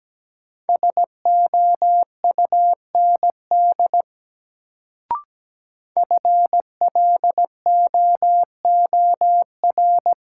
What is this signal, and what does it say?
Morse code, heard as a keyed tone, about 17 wpm, 700 Hz. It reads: SOUND FLOOR
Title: Sets of 3 Words - New Word Drill - 17wpm